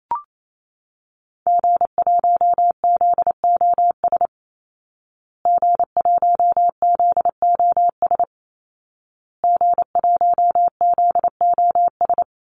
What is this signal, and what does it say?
Morse code, heard as a keyed tone, about 28 wpm, 700 Hz. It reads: G1ZOS G1ZOS G1ZOS
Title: Call Signs Repeated 2x - 28wpm